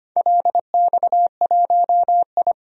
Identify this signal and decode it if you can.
Morse code, heard as a keyed tone, about 25 wpm, 700 Hz. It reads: LX1I
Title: DXCC-100 Call Signs - 4 Characters - 25wpm